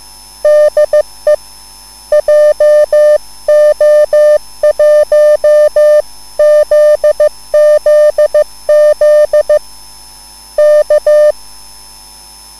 Morse code, heard as a keyed tone, about 15 wpm, 600 Hz. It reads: DE JO1ZZZ K
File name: cwbook